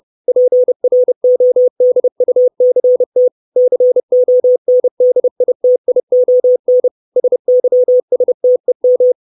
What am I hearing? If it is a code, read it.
Morse code, heard as a keyed tone, about 30 wpm, 500 Hz. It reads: PRODUCT CONDITION SYSTEM
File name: 632 Words 30WPM